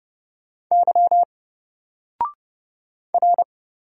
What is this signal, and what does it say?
Morse code, heard as a keyed tone, about 30 wpm, 700 Hz. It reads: Y R
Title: Single Character (Letter:Number:Punctuation:Prosign) 30wpm